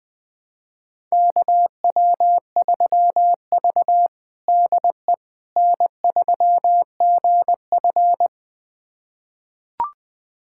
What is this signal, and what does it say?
Morse code, heard as a keyed tone, about 21 wpm, 700 Hz. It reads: KW3V DE N3GF
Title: USA Callsign DE Callsign - 4 Characters - 20wpm